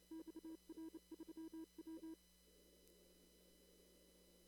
Morse code, heard as a keyed tone, about 30 wpm, 320 Hz. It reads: XR3W
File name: CW / kt5j wpx cw-op-k5pi